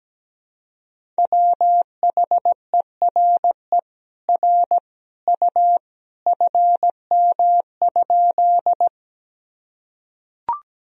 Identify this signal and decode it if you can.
Morse code, heard as a keyed tone, about 17 wpm, 700 Hz. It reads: WHERE R U FM?
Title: Sentences from Top 200 Words and Abbreviations 17wpm